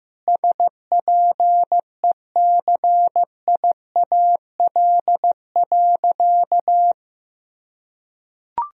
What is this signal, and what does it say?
Morse code, heard as a keyed tone, about 14 wpm, 700 Hz. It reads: SPECIAL.
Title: Sentences from Top 1000 Words - Review Part 3 - 15wpm